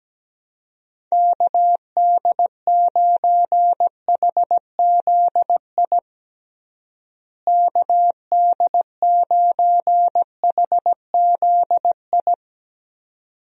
Morse code, heard as a keyed tone, about 17 wpm, 700 Hz. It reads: KD9HZI KD9HZI
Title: USA Call Signs - 6 Characters - Repeated 2x - 17wpm